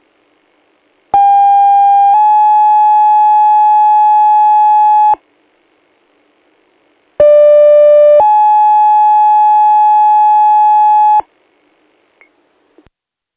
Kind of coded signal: two-tone page